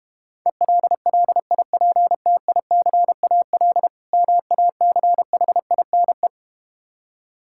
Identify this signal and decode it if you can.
Morse code, heard as a keyed tone, about 32 wpm, 700 Hz. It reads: ELLIPTICAL MACHINE